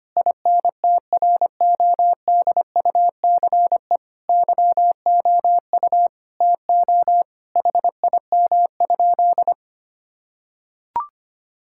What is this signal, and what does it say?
Morse code, heard as a keyed tone, about 25 wpm, 700 Hz. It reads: INTRODUCE YOU TO HIM?
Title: Sentences from Top 1600 Words 25wpm